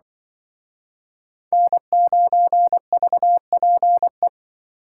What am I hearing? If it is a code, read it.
Morse code, heard as a keyed tone, about 25 wpm, 700 Hz. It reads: N9VPE